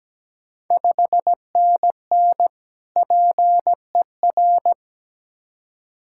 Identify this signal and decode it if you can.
Morse code, heard as a keyed tone, about 17 wpm, 700 Hz. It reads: 5NN PER
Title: Ontario QSO Party Exchange 17wpm